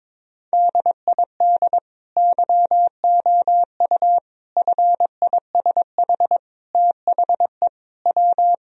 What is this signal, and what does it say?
Morse code, heard as a keyed tone, about 22 wpm, 700 Hz. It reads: DID YOU FISH THE W
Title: Sentences from Top 1000 Words - Review Part 1 - 22wpm